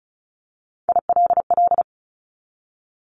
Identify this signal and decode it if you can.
Morse code, heard as a keyed tone, about 34 wpm, 700 Hz. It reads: ILL
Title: Top 1751-2000 Words 35wpm